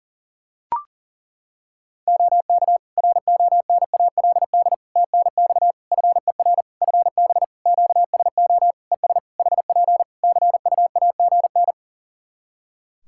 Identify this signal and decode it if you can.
Morse code, heard as a keyed tone, about 37 wpm, 700 Hz. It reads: OK RONALD TNX FER FB QSO ES HP CUAGN
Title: Practice QSO - Exchange 4 - Ending - 40wpm